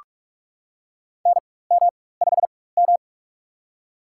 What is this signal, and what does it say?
Morse code, heard as a keyed tone, about 46 wpm, spaced as for 21 wpm, 700 Hz. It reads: NM5M
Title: USA Call Signs - 4 Characters - 4x Character Spacing - 45wpm